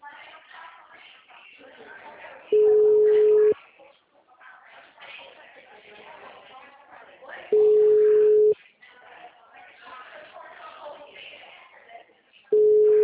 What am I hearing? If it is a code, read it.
Ringback tone (Europe, 425 Hz), with speech.
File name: out- -2010- - - .13863